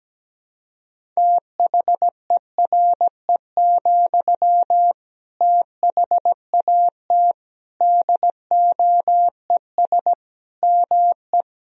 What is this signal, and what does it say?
Morse code, heard as a keyed tone, about 17 wpm, 700 Hz. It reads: THERE, THAT DOES ME
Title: Sentences from Top 200 Words and Abbreviations 17wpm